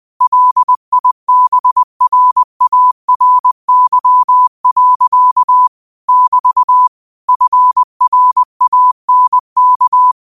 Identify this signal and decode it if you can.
Morse code, heard as a keyed tone, about 20 wpm, 1000 Hz. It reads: LIBRARY. <BT> FRANK
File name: qotdfiles / 20wpm